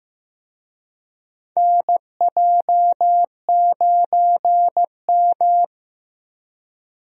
Morse code, heard as a keyed tone, about 15 wpm, 700 Hz. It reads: NJ9M
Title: USA Call Signs - 4 Characters - Repeated 3x - 15wpm